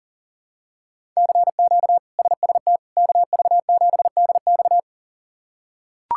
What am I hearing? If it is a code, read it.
Morse code, heard as a keyed tone, about 40 wpm, 700 Hz. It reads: CQ SST KV7DX
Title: K1USN SST QSO Practice - 40wpm